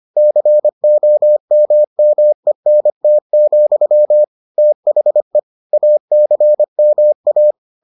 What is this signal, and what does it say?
Morse code, heard as a keyed tone, about 25 wpm, 600 Hz. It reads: COMMENT, THE ACMA